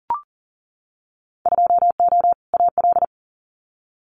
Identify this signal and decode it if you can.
Morse code, heard as a keyed tone, about 41 wpm, 700 Hz. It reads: 2O AL
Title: Winter Field Day Exchanges Encore 40wpm